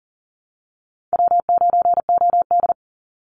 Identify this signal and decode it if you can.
Morse code, heard as a keyed tone, about 39 wpm, 700 Hz. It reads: W9OD
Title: Easy Call Signs - 40wpm